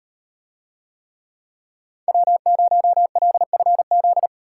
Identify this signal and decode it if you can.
Morse code, heard as a keyed tone, about 38 wpm, 700 Hz. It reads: W0LFZ